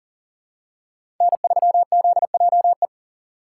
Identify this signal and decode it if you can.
Morse code, heard as a keyed tone, about 37 wpm, 690 Hz. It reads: N3ZJE
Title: USA Call Signs - 5 Characters - 3x Character Spacing - 40wpm